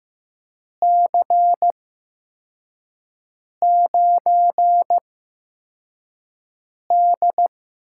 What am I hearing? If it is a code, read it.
Morse code, heard as a keyed tone, about 15 wpm, spaced as for 4 wpm, 700 Hz. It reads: C9D
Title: USA Call Signs - 4 Characters - 8x Character Spacing - 15wpm